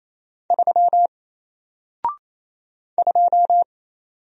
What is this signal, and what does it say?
Morse code, heard as a keyed tone, about 27 wpm, 700 Hz. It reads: 3 2
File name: Single-Letter-Number-28wpm